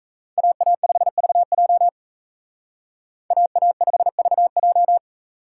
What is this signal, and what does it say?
Morse code, heard as a keyed tone, about 32 wpm, 700 Hz. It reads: AA5VJ AA5VJ
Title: Speed Racing - USA 2x2 Call Signs 28wpm